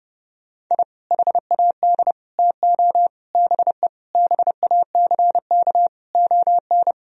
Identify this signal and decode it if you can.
Morse code, heard as a keyed tone, about 32 wpm, 700 Hz. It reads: I HAD TO BE BACK ON
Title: Sentences from Top 1000 Words - Review Part 4 - 30wpm